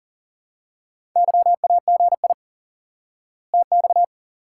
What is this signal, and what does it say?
Morse code, heard as a keyed tone, about 40 wpm, 700 Hz. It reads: YAGI TX
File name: IQE Course 3 - Lesson 89 - YAGI - Two Elements - 40wpm